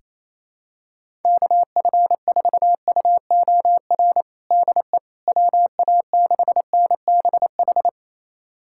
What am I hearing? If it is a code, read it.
Morse code, heard as a keyed tone, about 28 wpm, 700 Hz. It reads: KF4UOR DE WA6NBH
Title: USA Callsign DE Callsign - 6 Characters - 28wpm